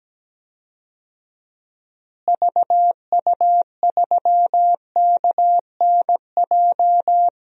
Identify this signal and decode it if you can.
Morse code, heard as a keyed tone, about 17 wpm, 700 Hz. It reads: VU3KNJ
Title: DXCC-100 Call Signs - 6 Characters - Repeated 2x - 17wpm